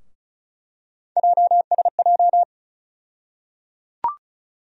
Morse code, heard as a keyed tone, about 36 wpm, 700 Hz. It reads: JSJ
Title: USA Call Sign Postfixes - 3 Letters 35wpm